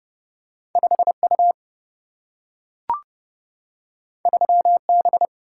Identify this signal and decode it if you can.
Morse code, heard as a keyed tone, about 30 wpm, 700 Hz. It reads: 5U 3B
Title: Single-Letter Single-Number 30wpm